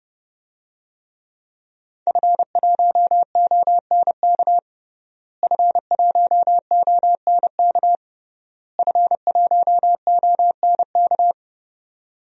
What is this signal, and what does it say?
Morse code, heard as a keyed tone, about 30 wpm, 700 Hz. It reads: F1ONK F1ONK F1ONK
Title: DXCC-100 Call Signs - 5 Characters - Repeated 3x - 30wpm